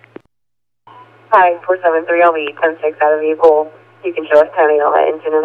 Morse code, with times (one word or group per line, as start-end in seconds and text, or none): none